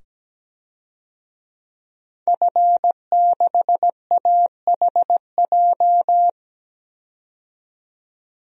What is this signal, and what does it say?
Morse code, heard as a keyed tone, about 17 wpm, 700 Hz. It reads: F6AHJ
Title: DXCC-100 Call Signs - 5 Characters - 17wpm